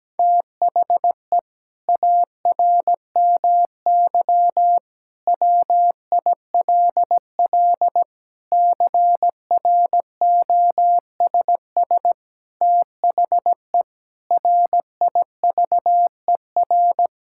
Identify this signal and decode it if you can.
Morse code, heard as a keyed tone, about 17 wpm, 700 Hz. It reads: THE ARMY WILL CROSS THE RIVER
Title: Sentences from Top 1000 Words - Review Part 5 - 17wpm